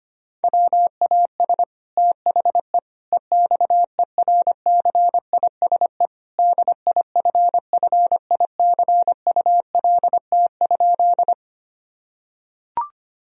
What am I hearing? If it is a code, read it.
Morse code, heard as a keyed tone, about 25 wpm, 700 Hz. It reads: WAS THE EXERCISE DIFFICULT?
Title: Sentences from Top 1000 Words - Review Part 4 - 25wpm